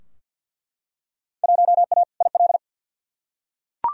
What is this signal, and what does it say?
Morse code, heard as a keyed tone, about 50 wpm, 700 Hz. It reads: 1A IL
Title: Field Day Exchanges Encore - 50wpm